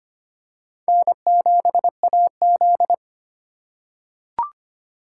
Morse code, heard as a keyed tone, about 25 wpm, 700 Hz. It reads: N7AZ
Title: K1USN SST QSO Practice - 25wpm